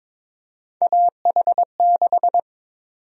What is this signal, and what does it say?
Morse code, heard as a keyed tone, about 22 wpm, 700 Hz. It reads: AH6
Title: USA Call Sign Prefixes - 22wpm